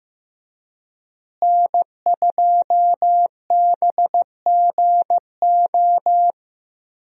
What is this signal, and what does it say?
Morse code, heard as a keyed tone, about 15 wpm, 700 Hz. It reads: N2BGO
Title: USA Call Signs - 5 Characters - Repeated 3x - 15wpm